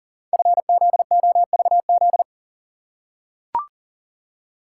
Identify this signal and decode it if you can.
Morse code, heard as a keyed tone, about 40 wpm, 700 Hz. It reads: FZOVZ
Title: Code Groups - 5 Letters - 40wpm